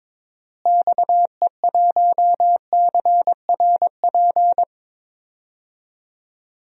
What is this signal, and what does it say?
Morse code, heard as a keyed tone, about 21 wpm, 700 Hz. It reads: XE1CRP
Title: DXCC-100 Call Signs - 6 Characters - Repeated 3x - 22wpm